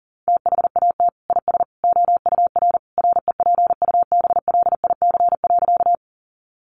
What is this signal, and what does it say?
Morse code, heard as a keyed tone, about 40 wpm, 700 Hz. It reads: THAT IS OUR REPUBLIC.
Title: Sentences from Top 1500 Words 40wpm